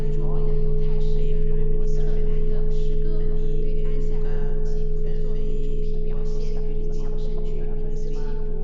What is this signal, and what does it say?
Dial tone (Europe, 425 Hz), with speech.